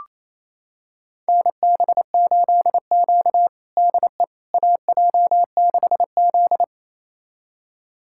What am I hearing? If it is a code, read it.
Morse code, heard as a keyed tone, about 28 wpm, 700 Hz. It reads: NB8Q DE AJ6Z
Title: USA Callsign DE Callsign - 4 Characters - 28wpm